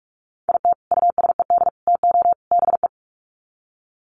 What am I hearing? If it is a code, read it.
Morse code, heard as a keyed tone, about 44 wpm, 700 Hz. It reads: IT USED TO BE
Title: Sets of 4 Words 45wpm